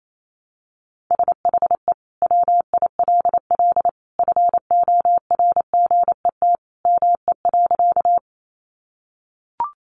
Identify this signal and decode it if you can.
Morse code, heard as a keyed tone, about 28 wpm, 700 Hz. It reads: SHE WILL FORGET ME.
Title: Sentences from Top 1200 Words 28wpm